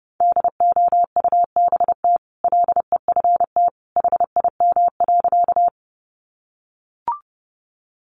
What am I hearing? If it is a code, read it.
Morse code, heard as a keyed tone, about 28 wpm, 700 Hz. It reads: DOUBT LEFT HIM.
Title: Sentences from Top 1500 Words 30wpm